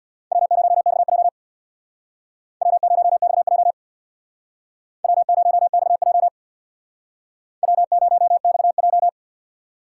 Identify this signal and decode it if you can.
Morse code, heard as a keyed tone, about 62 wpm, 690 Hz. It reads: W0XJ W0XJ W0XJ W0XJ
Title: Speed Racing - USA 1x2 Call Signs 50wpm